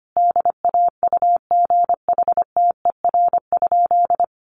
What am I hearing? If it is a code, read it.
Morse code, heard as a keyed tone, about 25 wpm, 700 Hz. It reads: DAUGHTER?
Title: Sentences from Top 1500 Words 25wpm